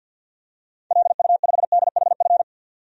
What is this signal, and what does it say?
Morse code, heard as a keyed tone, about 53 wpm, 700 Hz. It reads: LU5DHF
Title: DXCC-100 Call Signs - 6 Characters - Repeated 3x - 50wpm